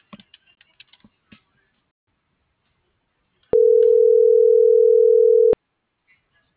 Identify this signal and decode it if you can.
Ringback tone (North America), 440 + 480 Hz.